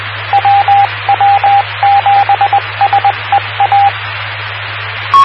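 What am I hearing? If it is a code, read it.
Morse code, heard as a keyed tone, about 20 wpm, 800 Hz, with radio noise. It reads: WW7SEA
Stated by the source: WW7PSR repeater ident on 146.960 MHz, PSRG.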